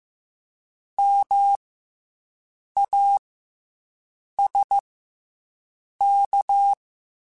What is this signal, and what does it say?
Morse code, heard as a keyed tone, about 15 wpm, 800 Hz. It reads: M A S K
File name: Special INGs-mp3s / Special INGs-mp3s / Fast ing @13